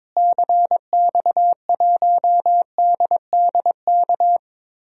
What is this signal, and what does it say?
Morse code, heard as a keyed tone, about 22 wpm, 700 Hz. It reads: CX1DDK
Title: DXCC-100 Call Signs - 6 Characters - Repeated 3x - 22wpm